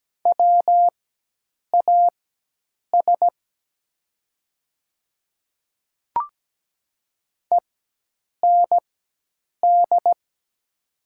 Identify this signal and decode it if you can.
Morse code, heard as a keyed tone, about 17 wpm, spaced as for 8 wpm, 700 Hz. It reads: WAS END
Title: 4x Character Spacing - Top 200 Words - 17wpm